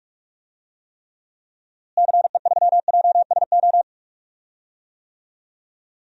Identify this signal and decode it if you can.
Morse code, heard as a keyed tone, about 44 wpm, 700 Hz. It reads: CE3JSO
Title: DXCC-100 Call Signs - 6 Characters - Repeated 2x - 45wpm